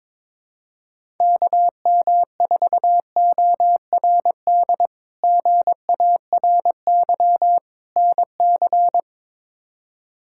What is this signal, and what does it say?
Morse code, heard as a keyed tone, about 21 wpm, 700 Hz. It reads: KM4ORD GARY NC
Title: K1USN SST QSO Practice - 22wpm